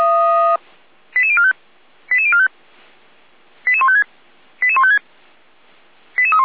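Five-tone selective call (ZVEI), heard as quick runs of five tones, tones of about 70 ms. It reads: 88045 88045 88156 88156
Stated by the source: SDR-received ZVEI five-tone selective call.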